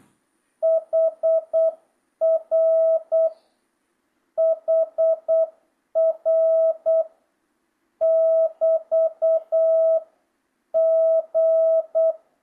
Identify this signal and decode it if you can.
Morse code, heard as a keyed tone, about 8 wpm, 650 Hz. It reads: HR HR <BT> G
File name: jh8cbh / 2025cw